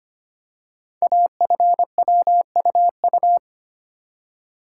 Morse code, heard as a keyed tone, about 25 wpm, 700 Hz. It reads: AFWUU
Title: Code Groups - 5 Letters - 25wpm